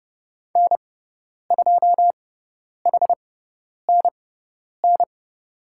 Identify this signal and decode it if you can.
Morse code, heard as a keyed tone, about 30 wpm, spaced as for 10 wpm, 700 Hz. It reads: N2HNN
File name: Lesson 18g - Letter m - Call Signs 30wpm